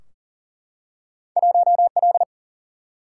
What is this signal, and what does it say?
Morse code, heard as a keyed tone, about 42 wpm, 700 Hz. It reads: 1L
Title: Single-Letter Single-Number 40wpm